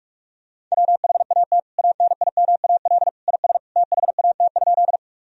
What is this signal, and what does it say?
Morse code, heard as a keyed tone, about 46 wpm, 700 Hz. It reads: WHAT ANIMAL IS THAT?
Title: Sentences from Top 1000 Words - Review Part 1 - 45wpm